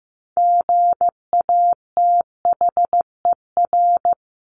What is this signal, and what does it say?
Morse code, heard as a keyed tone, about 15 wpm, 700 Hz. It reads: GATHER
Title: Top 1501-1750 Words 15wpm